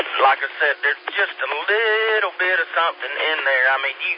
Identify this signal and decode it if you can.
Emergency Alert System (SAME) tones, heard as harsh buzzing bursts, with speech. It is heard from 1.0 to 3.9 s.